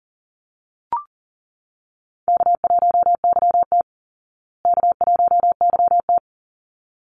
Morse code, heard as a keyed tone, about 40 wpm, 700 Hz. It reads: K1YT K1YT
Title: Call Signs Repeated 3x - 40wpm